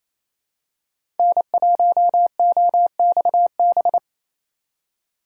Morse code, heard as a keyed tone, about 28 wpm, 700 Hz. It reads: N1OXB
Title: USA Call Signs - 5 Characters - 4x Character Spacing - 28wpm